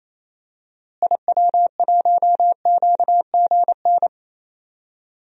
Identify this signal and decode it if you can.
Morse code, heard as a keyed tone, about 27 wpm, 700 Hz. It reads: IW1QGN
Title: DXCC-100 Call Signs - 6 Characters - Repeated 2x - 28wpm